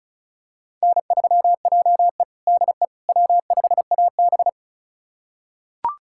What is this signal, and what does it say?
Morse code, heard as a keyed tone, about 36 wpm, 700 Hz. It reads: N3JE DE W5AB
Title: USA Callsign DE Callsign - 4 Characters - 35wpm